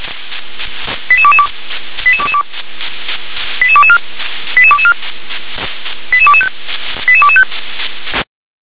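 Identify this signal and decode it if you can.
Five-tone selective call (ZVEI), heard as quick runs of five tones, tones of about 70 ms. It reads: 88222 88222 88224 88224 88225 88225
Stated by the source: ZVEI five-tone selective call received by SDR.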